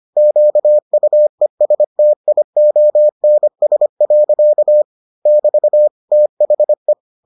Morse code, heard as a keyed tone, about 25 wpm, 600 Hz. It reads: QUESTIONS. <BT> THE